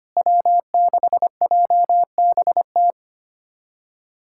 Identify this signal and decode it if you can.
Morse code, heard as a keyed tone, about 25 wpm, 700 Hz. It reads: W6JBT
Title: USA Callsign DE Callsign - 5 Characters - 25wpm